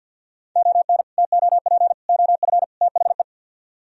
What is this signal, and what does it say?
Morse code, heard as a keyed tone, about 66 wpm, spaced as for 57 wpm, 700 Hz. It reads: ON TOP OF THE
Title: Speed-Racing - Sets of 4 Words 50wpm